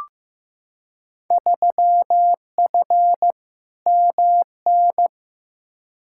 Morse code, heard as a keyed tone, about 15 wpm, 700 Hz. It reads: 3F MN
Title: Field Day Exchanges Encore - 15wpm